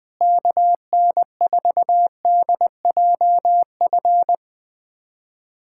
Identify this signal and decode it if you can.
Morse code, heard as a keyed tone, about 20 wpm, 700 Hz. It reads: KN4DJF
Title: USA Callsign DE Callsign - 6 Characters - 20wpm